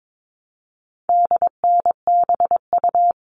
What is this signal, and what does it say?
Morse code, heard as a keyed tone, about 22 wpm, 700 Hz. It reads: DNBU
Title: All Cut Numbers - 4-Digit 22wpm